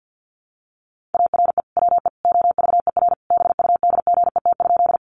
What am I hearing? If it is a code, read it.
Morse code, heard as a keyed tone, about 50 wpm, 700 Hz. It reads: ARE WE OVER BUDGET?
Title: Sentences from Top 1200 Words 50wpm